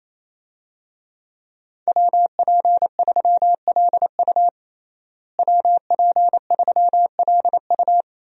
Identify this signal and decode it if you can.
Morse code, heard as a keyed tone, about 27 wpm, 700 Hz. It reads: WP3LU WP3LU
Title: USA Call Signs - 5 Characters - Repeated 2x - 28wpm